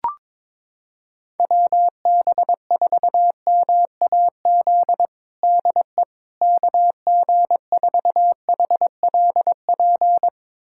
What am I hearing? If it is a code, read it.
Morse code, heard as a keyed tone, about 21 wpm, 700 Hz. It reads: WB4MAZ DE KG4HLP
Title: USA Callsign DE Callsign - 6 Characters - 22wpm